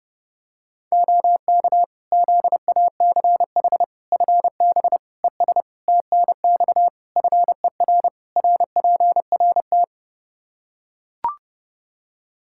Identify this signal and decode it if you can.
Morse code, heard as a keyed tone, about 32 wpm, 700 Hz. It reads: OK ZACH FB ES TNX FER RPRT
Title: K1USN SST QSO Practice 30wpm